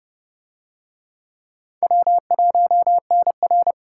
Morse code, heard as a keyed tone, about 29 wpm, 700 Hz. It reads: W1NR